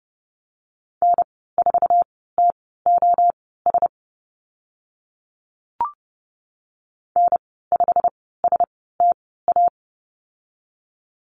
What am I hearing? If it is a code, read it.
Morse code, heard as a keyed tone, about 30 wpm, spaced as for 17 wpm, 690 Hz. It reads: N4TOS N5STA
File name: Lesson 15g - Number 5 - Call Signs 30wpm